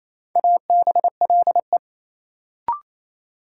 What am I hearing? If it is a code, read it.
Morse code, heard as a keyed tone, about 28 wpm, 700 Hz. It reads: ABLE
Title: Sentences from Top 700 Words - New Word Drill - 28wpm